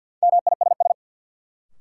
Morse code, heard as a keyed tone, about 47 wpm, 700 Hz. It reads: MISS
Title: Sentences from Top 500 Words - New Word Drill - 50wpm